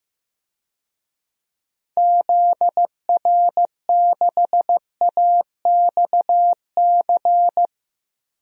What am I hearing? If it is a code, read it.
Morse code, heard as a keyed tone, about 15 wpm, 700 Hz. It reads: ZR6AXC
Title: DXCC-100 Call Signs - 6 Characters - 15wpm